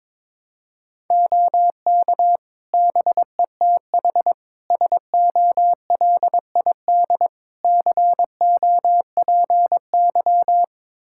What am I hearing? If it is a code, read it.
Morse code, heard as a keyed tone, about 22 wpm, 700 Hz. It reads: OK BETH SOLID COPY
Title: Practice QSO - Exchange 3 - More Info - 22wpm